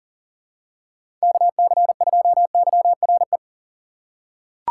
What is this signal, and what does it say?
Morse code, heard as a keyed tone, about 40 wpm, 700 Hz. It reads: KC2YRE
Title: USA Call Signs - 6 Characters - 4x Character Spacing - 40wpm